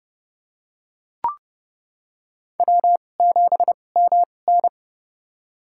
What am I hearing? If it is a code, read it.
Morse code, heard as a keyed tone, about 32 wpm, spaced as for 22 wpm, 700 Hz. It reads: W7MN
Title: USA Call Signs - 4 Characters - 2x Character Spacing - 30wpm